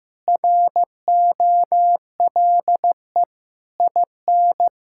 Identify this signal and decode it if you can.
Morse code, heard as a keyed tone, about 16 wpm, 700 Hz. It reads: ROLE IN
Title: Sentences from Top 1000 Words - Review Part 6 - 15wpm